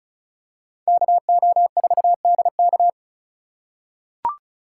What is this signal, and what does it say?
Morse code, heard as a keyed tone, about 36 wpm, 700 Hz. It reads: KO4DK
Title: USA Call Signs - 5 Characters - 3x Character Spacing - 35wpm